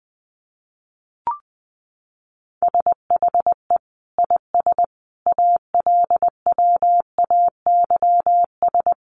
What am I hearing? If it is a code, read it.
Morse code, heard as a keyed tone, about 20 wpm, 700 Hz. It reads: SHE IS ALWAYS
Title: Sentences from Top 1200 Words 20wpm